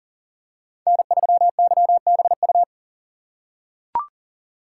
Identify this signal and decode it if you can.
Morse code, heard as a keyed tone, about 40 wpm, 700 Hz. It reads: N3YBU
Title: USA Call Signs - 5 Characters - 4x Character Spacing - 40wpm